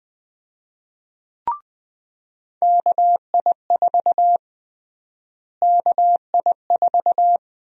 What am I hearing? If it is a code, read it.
Morse code, heard as a keyed tone, about 21 wpm, 700 Hz. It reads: KI4 KI4
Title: USA Call Sign Prefixes - Repeated 2x - 20wpm